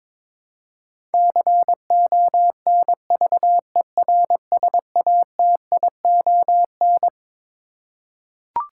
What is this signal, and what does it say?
Morse code, heard as a keyed tone, about 21 wpm, 700 Hz. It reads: CONVERSATION
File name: Twelve Letter Words - 22wpm